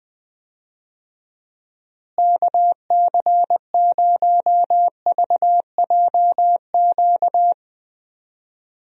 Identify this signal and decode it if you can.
Morse code, heard as a keyed tone, about 20 wpm, 700 Hz. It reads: KC0VJQ
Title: USA Call Signs - 6 Characters - Repeated 3x - 20wpm